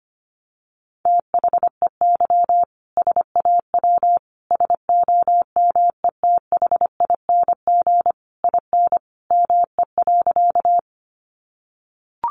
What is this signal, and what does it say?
Morse code, heard as a keyed tone, about 25 wpm, 700 Hz. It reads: THEY SAW SOMETHING IN ME.
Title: Sentences from Top 600 Words 25wpm